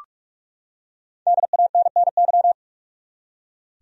Morse code, heard as a keyed tone, about 41 wpm, 700 Hz. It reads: DANNY
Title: USA Common Names - Max 5 Letters - 45wpm